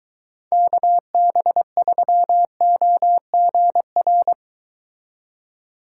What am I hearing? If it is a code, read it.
Morse code, heard as a keyed tone, about 23 wpm, 700 Hz. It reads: KB3OGR